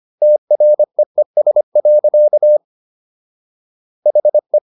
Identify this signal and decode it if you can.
Morse code, heard as a keyed tone, about 25 wpm, 600 Hz. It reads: TREES. HE